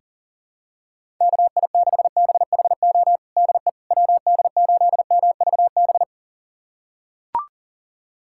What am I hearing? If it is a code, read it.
Morse code, heard as a keyed tone, about 40 wpm, 700 Hz. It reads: KI6BHO DE WD8MVB
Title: USA Callsign DE Callsign 40wpm